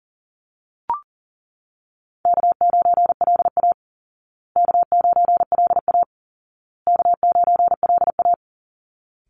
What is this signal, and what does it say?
Morse code, heard as a keyed tone, about 40 wpm, 700 Hz. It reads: K9LA K9LA K9LA
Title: Easy Call Signs Repeated 3x - 40wpm